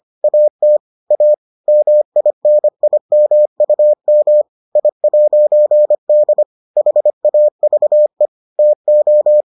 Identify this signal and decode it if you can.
Morse code, heard as a keyed tone, about 25 wpm, 600 Hz. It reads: AT A MINIMUM I'D HAVE TO